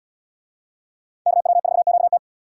Morse code, heard as a keyed tone, about 71 wpm, 700 Hz. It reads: KW4QT